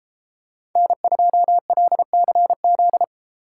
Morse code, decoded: N2LCZ

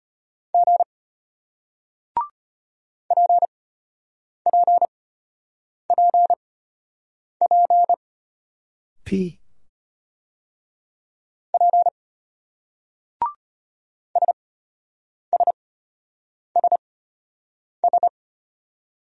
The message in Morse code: G PPPP P SSSS

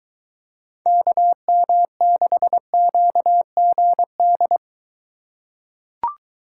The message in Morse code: KM6QGD

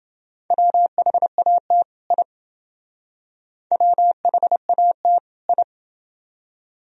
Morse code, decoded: WHAT I WHAT I

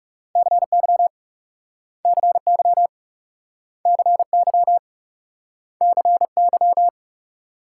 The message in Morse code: CY CY CY CY

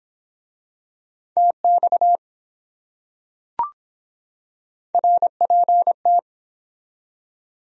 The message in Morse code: TX RPT

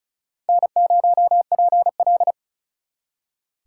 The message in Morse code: N0PL

